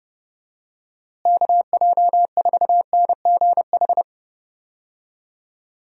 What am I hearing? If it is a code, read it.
Morse code, heard as a keyed tone, about 30 wpm, 700 Hz. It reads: KJ4NGH
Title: USA Call Signs - 6 Characters - Repeated 2x - 30wpm